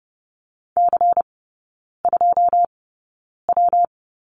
Morse code, heard as a keyed tone, about 30 wpm, 700 Hz. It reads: C 2 W
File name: Lesson 19e - Letter w - Three Characters 30wpm